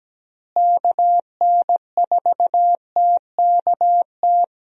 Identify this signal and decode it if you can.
Morse code, heard as a keyed tone, about 17 wpm, 700 Hz. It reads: KN4TKT
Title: USA Call Signs - 6 Characters - Repeated 3x - 17wpm